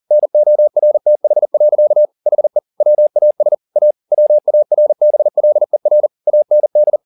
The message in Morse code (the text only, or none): NORTH. HE WAS A WARBLER AND